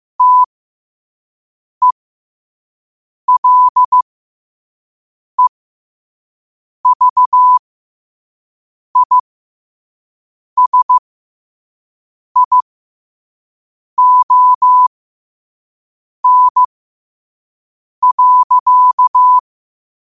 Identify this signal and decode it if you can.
Morse code, heard as a keyed tone, about 15 wpm, spaced as for 5 wpm, 1000 Hz. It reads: TELEVISION.
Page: New quotes every day in morse code at 5 Words per minute.